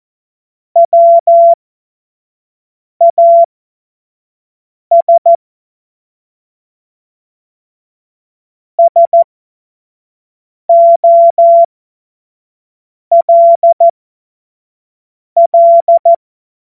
Morse code, heard as a keyed tone, about 14 wpm, spaced as for 5 wpm, 680 Hz. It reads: WAS SOLL
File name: Morsekurs / Witze / Witz 04. Mai 2021, 25 F70